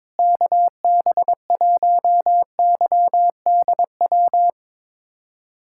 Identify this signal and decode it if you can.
Morse code, heard as a keyed tone, about 22 wpm, 700 Hz. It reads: KB1YDW
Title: USA Callsign DE Callsign - 6 Characters - 22wpm